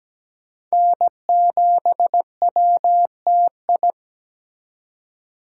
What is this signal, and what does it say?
Morse code, heard as a keyed tone, about 20 wpm, 700 Hz. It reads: N7WTI
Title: Speed Racing - USA 1x3 Call Signs 17wpm